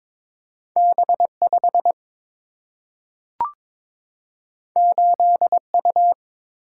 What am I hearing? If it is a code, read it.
Morse code, heard as a keyed tone, about 22 wpm, 700 Hz. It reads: B5 8U
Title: Single-Letter Single-Number 22wpm